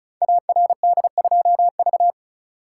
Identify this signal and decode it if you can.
Morse code, heard as a keyed tone, about 36 wpm, 700 Hz. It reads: ARD2V
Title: Code Groups - 5 Letter:Numbers - 35wpm